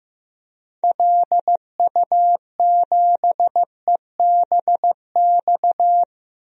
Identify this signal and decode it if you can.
Morse code, heard as a keyed tone, about 15 wpm, 700 Hz. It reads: LU7EBX
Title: DXCC-100 Call Signs - 6 Characters - 15wpm